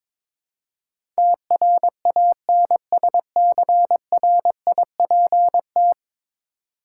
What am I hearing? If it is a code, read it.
Morse code, heard as a keyed tone, about 21 wpm, 700 Hz. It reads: TRANSCRIPT
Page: morse-code-ninja-practice-22wpm